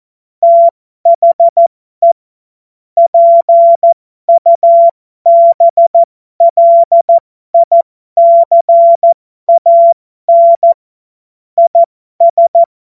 Morse code, heard as a keyed tone, about 14 wpm, spaced as for 12 wpm, 680 Hz. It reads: THE PUBLICAN IS
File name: Morsekurs / Witze